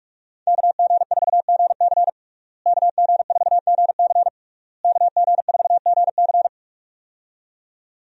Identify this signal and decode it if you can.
Morse code, heard as a keyed tone, about 46 wpm, 700 Hz. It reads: KG4GC KG4GC KG4GC
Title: USA Call Signs - 5 Characters - Repeated 3x - 45wpm